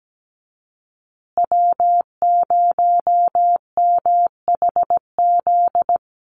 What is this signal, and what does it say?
Morse code, heard as a keyed tone, about 17 wpm, 700 Hz. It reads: W0MHZ
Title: USA Call Signs Repeated 3x 17wpm